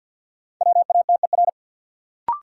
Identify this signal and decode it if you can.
Morse code, heard as a keyed tone, about 52 wpm, 700 Hz. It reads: WATER
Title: Sentences from Top 100 Words - New Word Drill - 50wpm